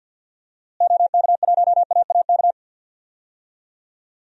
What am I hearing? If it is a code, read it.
Morse code, heard as a keyed tone, about 52 wpm, 700 Hz. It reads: OK1AAK
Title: DXCC-100 Call Signs - 6 Characters - Repeated 2x - 50wpm